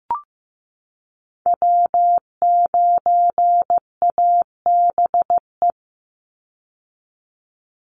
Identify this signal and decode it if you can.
Morse code, heard as a keyed tone, about 15 wpm, 700 Hz. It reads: W9ABE
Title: USA Call Signs 15wpm